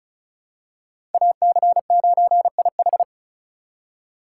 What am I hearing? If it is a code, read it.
Morse code, heard as a keyed tone, about 36 wpm, 700 Hz. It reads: AC9IH
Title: USA Call Signs - 5 Characters - 2x Character Spacing - 35wpm